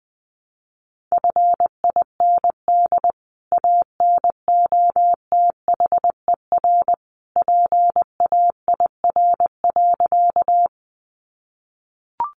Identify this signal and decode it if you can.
Morse code, heard as a keyed tone, about 20 wpm, 700 Hz. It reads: FIND ANOTHER PAIR.
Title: Sentences from Top 1200 Words 20wpm